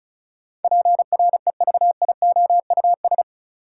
Morse code, heard as a keyed tone, about 36 wpm, 700 Hz. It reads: PREVIOUS
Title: Sentences from Top 1500 Words - New Word Drill Alt - 35wpm